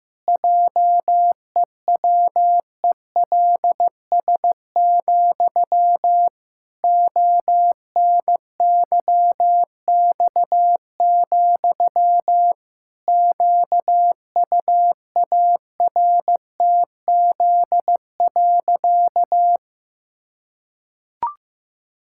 Morse code, heard as a keyed tone, about 15 wpm, 700 Hz. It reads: JEWELS, ONYX, QUARTZ.
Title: Pangram Challenge v2 - 15wpm